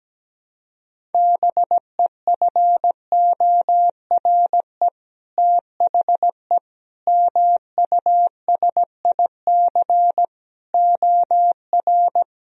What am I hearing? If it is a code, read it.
Morse code, heard as a keyed tone, about 17 wpm, 700 Hz. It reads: BEFORE THE MUSIC OR
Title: Sentences from Top 1000 Words - Review Part 1 - 17wpm